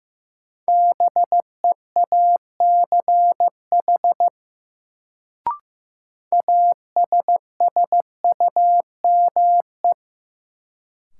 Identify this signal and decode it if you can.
Morse code, heard as a keyed tone, about 15 wpm, 700 Hz. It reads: BEACH ASSUME
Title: Sentences from Top 1500 Words - New Word Drill - 15wpm